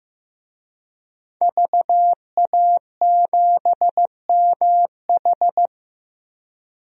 Morse code, heard as a keyed tone, about 15 wpm, 700 Hz. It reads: VA7MH
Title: DXCC-100 Call Signs - 5 Characters - 15wpm